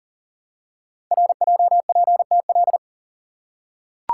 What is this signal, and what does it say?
Morse code, heard as a keyed tone, about 41 wpm, 700 Hz. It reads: RJPTL
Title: Code Groups - 5 Letters - 40wpm